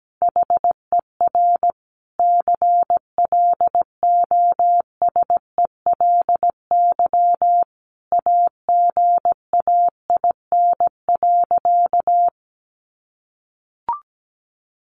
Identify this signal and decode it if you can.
Morse code, heard as a keyed tone, about 17 wpm, 700 Hz. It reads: HER CLOSELY AGAIN.
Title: Sentences from Top 1400 Words 17wpm